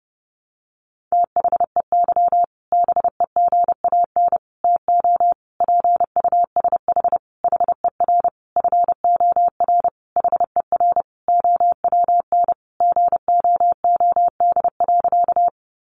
Morse code, heard as a keyed tone, about 30 wpm, 700 Hz. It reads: THEY BEGAN TO PUSH HER FOR HER OWN GOOD.
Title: Sentences from Top 1300 Words 30wpm